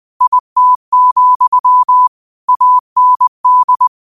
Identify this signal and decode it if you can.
Morse code, heard as a keyed tone, about 20 wpm, 1000 Hz. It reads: IT, AND
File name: qotdfiles / 20wpm